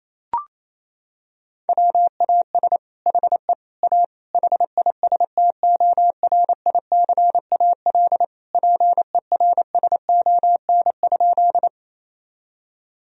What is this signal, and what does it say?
Morse code, heard as a keyed tone, about 28 wpm, 700 Hz. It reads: WAS HE A HISTORICAL PERSON?
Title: Sentences from Top 1000 Words - Review Part 4 - 28wpm